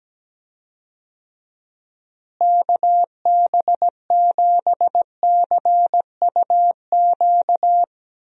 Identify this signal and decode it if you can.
Morse code, heard as a keyed tone, about 17 wpm, 700 Hz. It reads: KB7CUQ
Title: USA Call Signs - 6 Characters - Repeated 3x - 17wpm